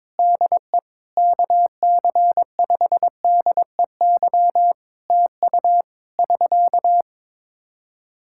Morse code, heard as a keyed tone, about 22 wpm, 700 Hz. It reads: DE KC5DEY TU <SK>
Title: Practice QSO - Exchange 4 - Ending - 22wpm